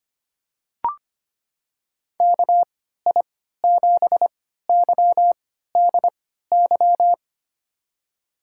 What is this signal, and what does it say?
Morse code, heard as a keyed tone, about 25 wpm, spaced as for 14 wpm, 700 Hz. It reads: KI7YDY
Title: USA Call Signs - 3x Character Spacing - 25wpm